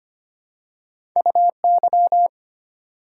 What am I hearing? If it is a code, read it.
Morse code, heard as a keyed tone, about 25 wpm, 700 Hz. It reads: UY
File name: Two Letter-Number Combinations - 25wpm